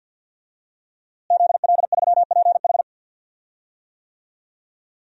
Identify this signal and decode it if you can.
Morse code, heard as a keyed tone, about 52 wpm, 700 Hz. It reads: ZL3PH
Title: DXCC-100 Call Signs - 5 Characters - Repeated 2x - 50wpm